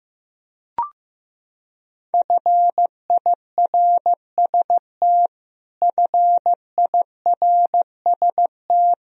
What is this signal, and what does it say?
Morse code, heard as a keyed tone, about 15 wpm, 700 Hz. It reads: FIRST FIRST
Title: Top 100 Words Repeated 15wpm